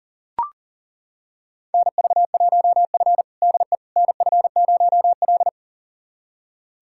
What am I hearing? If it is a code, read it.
Morse code, heard as a keyed tone, about 37 wpm, 700 Hz. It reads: NV1F DE NF0L
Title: USA Callsign DE Callsign - 4 Characters - 40wpm